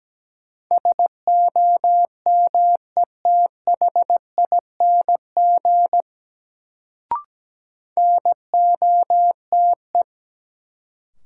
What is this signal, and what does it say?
Morse code, heard as a keyed tone, about 17 wpm, 700 Hz. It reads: SOMETHING NOTE
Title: Sets of 5 Words - New Word Drill - 17wpm